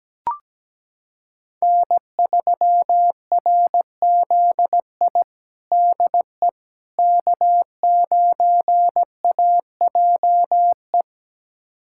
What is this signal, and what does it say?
Morse code, heard as a keyed tone, about 17 wpm, 700 Hz. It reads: N3RZI DE K9AJE
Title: USA Callsign DE Callsign - 5 Characters - 17wpm